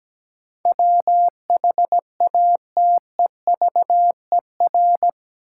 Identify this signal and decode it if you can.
Morse code, heard as a keyed tone, about 17 wpm, 700 Hz. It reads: WHATEVER
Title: Sentences from Top 1100 Words - New Word Drill - 17wpm